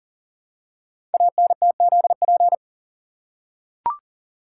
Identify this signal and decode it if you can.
Morse code, heard as a keyed tone, about 38 wpm, 700 Hz. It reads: ANTZP
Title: Code Groups - 5 Letters - 40wpm